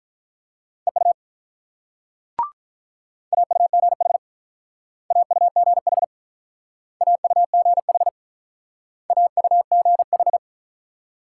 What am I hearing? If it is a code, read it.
Morse code, heard as a keyed tone, about 43 wpm, 690 Hz. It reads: EU AUGH AUGH AUGH AUGH
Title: Speed-Racing Multi-letter Phonemes - 35wpm